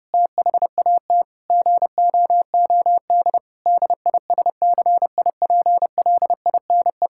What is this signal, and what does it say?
Morse code, heard as a keyed tone, about 30 wpm, 700 Hz. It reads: THAT GOOD DISCIPLINE
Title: Sentences from Top 1000 Words - Review Part 6 - 30wpm